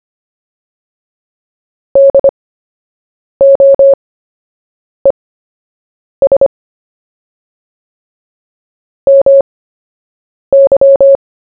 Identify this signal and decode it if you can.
Morse code, heard as a keyed tone, about 25 wpm, spaced as for 7 wpm, 550 Hz. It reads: DOES MY
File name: cw / fists / Short-Story-1-7wpm